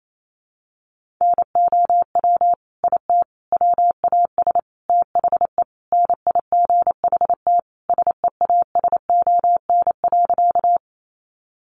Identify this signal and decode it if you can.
Morse code, heard as a keyed tone, about 28 wpm, 700 Hz. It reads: NOW IT WAS THE NIGHT SEASON.
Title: Sentences from Top 1400 Words 28wpm